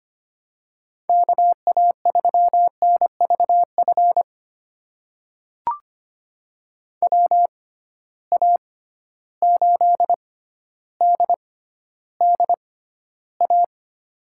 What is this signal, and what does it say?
Morse code, heard as a keyed tone, about 25 wpm, spaced as for 13 wpm, 700 Hz. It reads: KA3NVF WA8DDA
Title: USA Call Signs - 6x Character Spacing - 25wpm